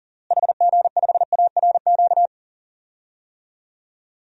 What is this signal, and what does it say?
Morse code, heard as a keyed tone, about 40 wpm, 700 Hz. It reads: HG5ARQ
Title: DXCC-100 Call Signs - 6 Characters - Repeated 2x - 40wpm